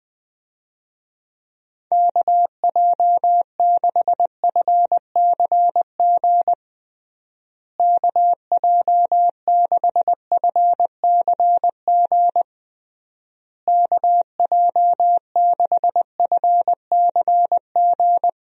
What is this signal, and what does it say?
Morse code, heard as a keyed tone, about 21 wpm, 700 Hz. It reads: KJ6FCG KJ6FCG KJ6FCG
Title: USA Call Signs - 6 Characters - Repeated 3x - 20wpm